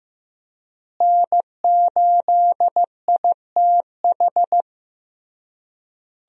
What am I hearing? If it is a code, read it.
Morse code, heard as a keyed tone, about 15 wpm, 700 Hz. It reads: N8ITH
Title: USA Call Signs - 5 Characters - Repeated 3x - 15wpm